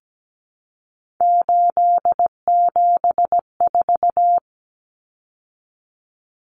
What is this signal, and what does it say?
Morse code, heard as a keyed tone, about 17 wpm, 700 Hz. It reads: 874
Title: Numbers - 3-Digit 17wpm